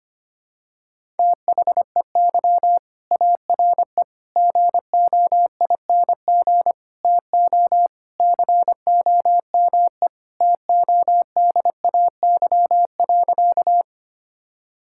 Morse code, heard as a keyed tone, about 25 wpm, 700 Hz. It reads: THEY ARE GOING TO COME TODAY.
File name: practice / 25wpm / Sentences with 3-Word Phrases - 25wpm